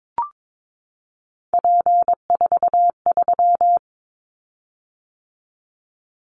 Morse code, heard as a keyed tone, about 22 wpm, 700 Hz. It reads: P43
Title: Callsign Prefixes for Top 100 DXCC - 22wpm